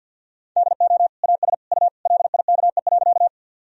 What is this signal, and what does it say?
Morse code, heard as a keyed tone, about 47 wpm, 700 Hz. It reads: DO AS U LIKE.
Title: Sentences from Top 100 Words and Abbreviations 50wpm